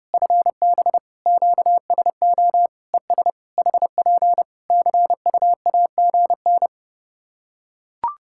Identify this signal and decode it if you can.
Morse code, heard as a keyed tone, about 30 wpm, 700 Hz. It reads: FB QSO ES HP CUAGN
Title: K1USN SST QSO Practice 30wpm